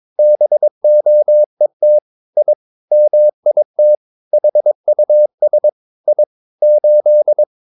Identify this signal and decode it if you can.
Morse code, heard as a keyed tone, about 22 wpm, 600 Hz. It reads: BOET I MIT HUS I 8
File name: Ferie QRS 22wpm